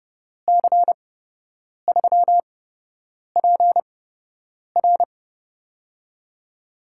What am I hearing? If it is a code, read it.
Morse code, heard as a keyed tone, about 30 wpm, spaced as for 8 wpm, 700 Hz. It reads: C3PR
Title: USA Call Signs - 5 Characters - 8x Character Spacing - 30wpm